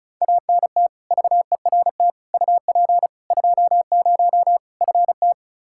Morse code, heard as a keyed tone, about 34 wpm, 700 Hz. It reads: ANT VERT UP 20 FT
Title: Practice QSO - Exchange 2 - Info - 35wpm